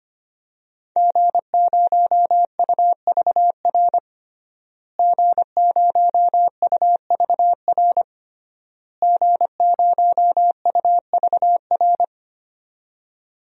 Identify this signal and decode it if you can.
Morse code, heard as a keyed tone, about 25 wpm, 700 Hz. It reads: G0UVR G0UVR G0UVR
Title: DXCC-100 Call Signs - 5 Characters - Repeated 3x - 25wpm